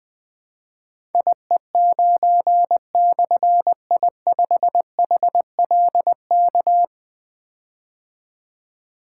Morse code, heard as a keyed tone, about 21 wpm, 700 Hz. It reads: IE9/I5HLK
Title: Difficult Call Signs 20wpm